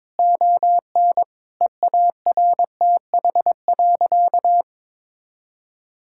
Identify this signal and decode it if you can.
Morse code, heard as a keyed tone, about 21 wpm, 700 Hz. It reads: ON EARTH.
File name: Difficult Sentences with Hints - 22wpm